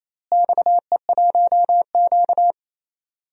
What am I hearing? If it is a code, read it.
Morse code, heard as a keyed tone, about 28 wpm, 700 Hz. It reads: XE1Q
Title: DXCC-100 Call Signs - 4 Characters - Repeated 2x - 28wpm